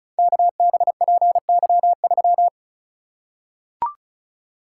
Morse code, heard as a keyed tone, about 36 wpm, 690 Hz. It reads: KBPY3
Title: Code Groups - 5 Letter:Numbers - 35wpm